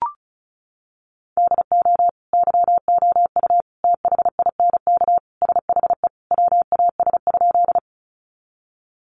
Morse code, heard as a keyed tone, about 34 wpm, 700 Hz. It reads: DO YOU THINK SHE WAS?
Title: Sentences from Top 200 Words 35wpm